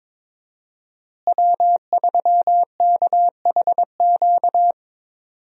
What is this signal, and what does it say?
Morse code, heard as a keyed tone, about 22 wpm, 700 Hz. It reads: W3KHQ
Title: USA Call Signs - 5 Characters - Repeated 2x - 22wpm